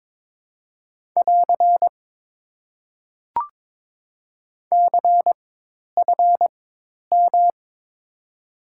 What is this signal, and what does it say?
Morse code, heard as a keyed tone, about 22 wpm, spaced as for 10 wpm, 700 Hz. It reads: <AR> CFM
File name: Top 90 QSO Elements - 4x Character Spacing - 22wpm